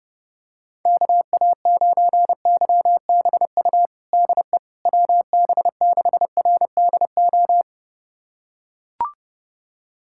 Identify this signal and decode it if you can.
Morse code, heard as a keyed tone, about 30 wpm, 700 Hz. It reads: KA9YBU DE WB6RDO
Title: USA Callsign DE Callsign 30wpm